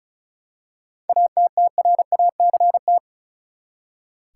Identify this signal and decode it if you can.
Morse code, heard as a keyed tone, about 36 wpm, 700 Hz. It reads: ATTRACT
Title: Seven Letter Words - 35wpm